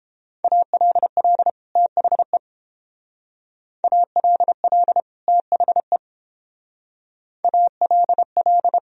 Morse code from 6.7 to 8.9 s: ALL

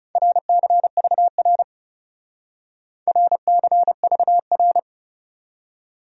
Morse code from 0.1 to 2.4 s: RCVR

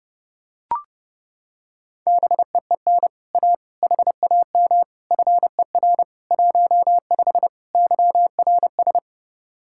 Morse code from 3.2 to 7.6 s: A HAM FER 15

